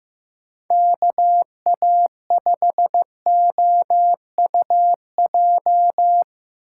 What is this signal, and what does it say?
Morse code, heard as a keyed tone, about 15 wpm, 700 Hz. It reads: KA5OUJ
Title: USA Call Signs - 6 Characters - 4x Character Spacing - 15wpm